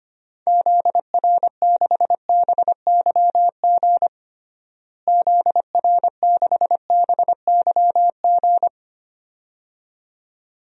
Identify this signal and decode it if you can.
Morse code, heard as a keyed tone, about 25 wpm, 700 Hz. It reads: ZR6BYG ZR6BYG
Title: DXCC-100 Call Signs - 6 Characters - Repeated 2x - 25wpm